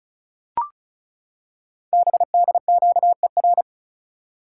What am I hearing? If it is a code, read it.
Morse code, heard as a keyed tone, about 36 wpm, 690 Hz. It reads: BDQER
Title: Code Groups - 5 Letters - 35wpm